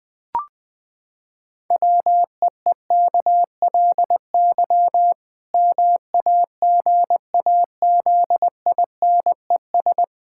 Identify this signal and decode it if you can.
Morse code, heard as a keyed tone, about 20 wpm, 700 Hz. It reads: WEEKLY MAGAZINES
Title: Pangram Challenge v2 - 20wpm